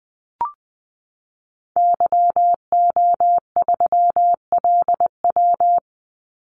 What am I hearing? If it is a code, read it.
Morse code, heard as a keyed tone, about 20 wpm, 700 Hz. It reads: YO3LW
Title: Call Signs Repeated 2x - 20wpm